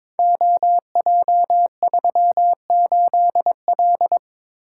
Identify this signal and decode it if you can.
Morse code, heard as a keyed tone, about 23 wpm, 700 Hz. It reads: OJ38L
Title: Code Groups - 5 Letter:Numbers - 22wpm